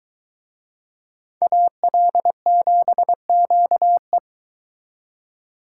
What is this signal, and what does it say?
Morse code, heard as a keyed tone, about 24 wpm, 700 Hz. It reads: AL7QE